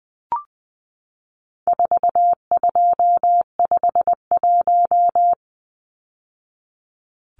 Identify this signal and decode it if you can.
Morse code, heard as a keyed tone, about 20 wpm, 700 Hz. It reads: 4251
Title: Numbers - 4-Digit 20wpm